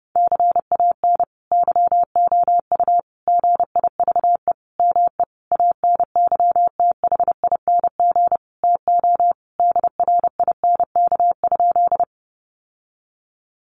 Morse code, heard as a keyed tone, about 30 wpm, 700 Hz. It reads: CAN YOU GIVE ME ANYTHING TO DRINK?
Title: Sentences from Top 1500 Words 30wpm